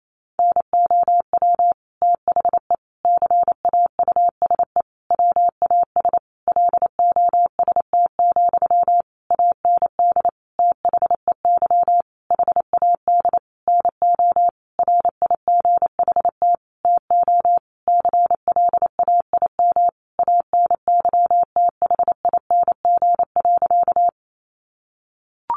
Morse code, heard as a keyed tone, about 28 wpm, 700 Hz. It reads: NOW THE CAUSE WAS LOST, AND THEY HAD NO RIGHT TO CLAIM ANYTHING.
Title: Sentences from Top 1500 Words 28wpm